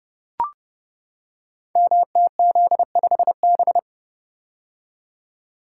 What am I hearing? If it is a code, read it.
Morse code, heard as a keyed tone, about 30 wpm, 700 Hz. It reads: MTZ5B
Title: Code Groups - 5 Letter:Numbers - 30wpm